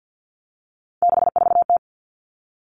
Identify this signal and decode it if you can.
Morse code, heard as a keyed tone, about 50 wpm, 700 Hz. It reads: 64T
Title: Common Cut Numbers - 3-Digit 50wpm